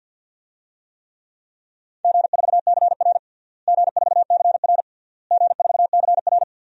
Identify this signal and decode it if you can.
Morse code, heard as a keyed tone, about 52 wpm, 700 Hz. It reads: G4CR G4CR G4CR
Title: DXCC-100 Call Signs - 4 Characters - Repeated 3x - 50wpm